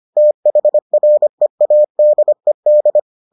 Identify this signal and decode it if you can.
Morse code, heard as a keyed tone, about 25 wpm, 600 Hz. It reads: THREADED